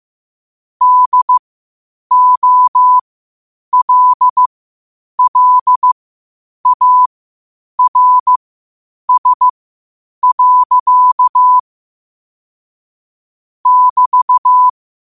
Morse code, heard as a keyed tone, about 15 wpm, spaced as for 8 wpm, 1000 Hz. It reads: DOLLARS. <BT>
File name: qotdfiles / 8wpm